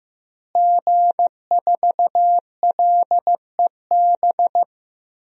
Morse code, heard as a keyed tone, about 16 wpm, 700 Hz. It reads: G4LEB
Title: DXCC-100 Call Signs - 5 Characters - Repeated 3x - 15wpm